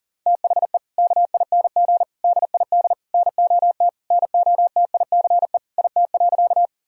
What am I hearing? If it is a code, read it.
Morse code, heard as a keyed tone, about 40 wpm, 700 Hz. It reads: THE KING DID NOT NOTICE IT.
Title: Sentences from Top 1000 Words - Review Part 2 - 40wpm